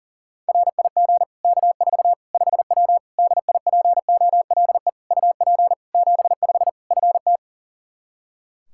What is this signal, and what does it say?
Morse code, heard as a keyed tone, about 37 wpm, 700 Hz. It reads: RIG K4 5W DIPOLE UP 75 FT
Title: Practice QSO - Exchange 2 - Info - 40wpm